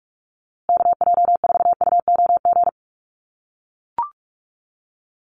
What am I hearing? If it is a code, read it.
Morse code, heard as a keyed tone, about 44 wpm, 700 Hz. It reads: KJ4UOG
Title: USA Call Signs Repeated 3x 45wpm